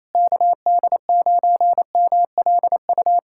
Morse code, heard as a keyed tone, about 27 wpm, 700 Hz. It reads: KD9MLU
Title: USA Callsign DE Callsign - 6 Characters - 28wpm